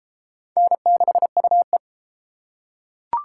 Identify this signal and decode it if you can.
Morse code, heard as a keyed tone, about 29 wpm, 700 Hz. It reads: N6UE